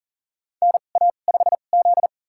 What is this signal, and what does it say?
Morse code, heard as a keyed tone, about 40 wpm, spaced as for 29 wpm, 700 Hz. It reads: NA5Z
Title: USA Call Signs - 4 Characters - 2x Character Spacing - 40wpm